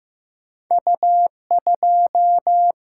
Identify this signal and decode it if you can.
Morse code, heard as a keyed tone, about 15 wpm, 700 Hz. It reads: U2
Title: Single-Letter Single-Number 15wpm